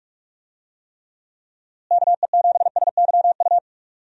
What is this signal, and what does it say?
Morse code, heard as a keyed tone, about 46 wpm, 700 Hz. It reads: KE7SYU
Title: USA Call Signs - 6 Characters - Repeated 2x - 45wpm